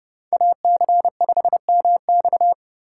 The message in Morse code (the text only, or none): AC5MX